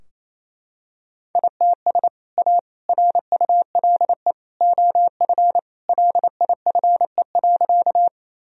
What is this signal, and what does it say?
Morse code, heard as a keyed tone, about 28 wpm, 700 Hz. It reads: ITS A RULE OF LIFE.
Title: Sentences from Top 1000 Words - Review Part 2 - 28wpm